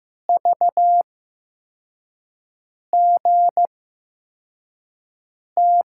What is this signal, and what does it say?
Morse code, heard as a keyed tone, about 15 wpm, spaced as for 4 wpm, 700 Hz. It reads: VGT
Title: USA Call Signs - 6 Characters - 8x Character Spacing - 15wpm